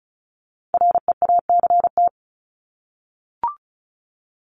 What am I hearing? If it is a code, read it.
Morse code, heard as a keyed tone, about 34 wpm, 700 Hz. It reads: REACT
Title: Top 1251-1500 Words 35wpm